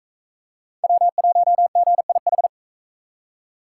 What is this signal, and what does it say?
Morse code, heard as a keyed tone, about 42 wpm, 700 Hz. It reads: W1GIH